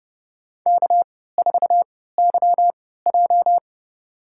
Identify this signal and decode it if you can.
Morse code, heard as a keyed tone, about 32 wpm, spaced as for 17 wpm, 700 Hz. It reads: K4YJ
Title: USA Call Signs - 4 Characters - 3x Character Spacing - 30wpm